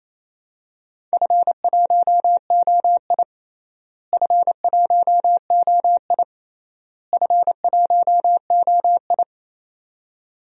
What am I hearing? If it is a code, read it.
Morse code, heard as a keyed tone, about 28 wpm, 700 Hz. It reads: F1OI F1OI F1OI
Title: DXCC-100 Call Signs - 4 Characters - Repeated 3x - 28wpm